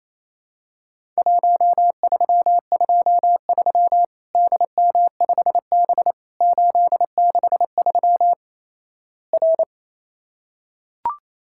Morse code, heard as a keyed tone, about 28 wpm, 700 Hz. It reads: 1323 DM5B 863 R
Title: Worked All Europe DX Contest - QTC Exchange - 28wpm